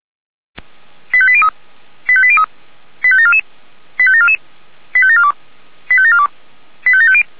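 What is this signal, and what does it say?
Five-tone selective call (ZVEI), heard as quick runs of five tones, tones of about 70 ms. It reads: 85803 85803 85744 85744 85742 85742 85750
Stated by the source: ZVEI five-tone call received by SDR.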